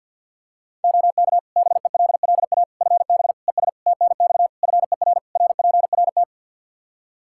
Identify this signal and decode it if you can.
Morse code, heard as a keyed tone, about 52 wpm, 700 Hz. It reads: OK BELLA FB ES TNX FER RPRT
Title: K1USN SST QSO Practice 50wpm